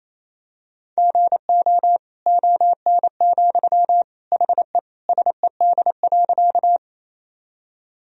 Morse code, heard as a keyed tone, about 28 wpm, 700 Hz. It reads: GO ON, HE SED.
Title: Sentences from Top 100 Words and Abbreviations 28wpm